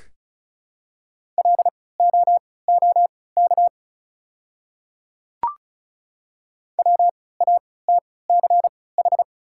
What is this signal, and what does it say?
Morse code, heard as a keyed tone, about 36 wpm, spaced as for 20 wpm, 700 Hz. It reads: LOOK WATCH
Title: 3x Character Spacing - Top 300 Words - 35wpm